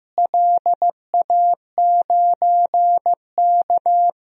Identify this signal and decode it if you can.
Morse code, heard as a keyed tone, about 15 wpm, 700 Hz. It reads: LA9K
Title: DXCC-100 Call Signs - 4 Characters - Repeated 2x - 15wpm